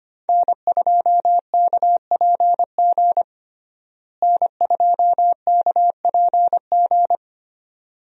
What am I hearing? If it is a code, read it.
Morse code, heard as a keyed tone, about 25 wpm, 700 Hz. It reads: N2KPG N2KPG
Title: USA Call Signs - 5 Characters - Repeated 2x - 25wpm